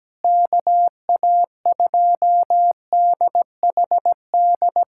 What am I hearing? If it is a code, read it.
Morse code, heard as a keyed tone, about 17 wpm, 700 Hz. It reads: KA2DHD
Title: USA Callsign DE Callsign - 6 Characters - 17wpm